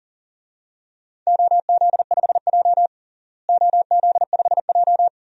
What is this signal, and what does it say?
Morse code, heard as a keyed tone, about 40 wpm, 700 Hz. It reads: OZ5J OZ5J
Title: DXCC-100 Call Signs - 4 Characters - Repeated 2x - 40wpm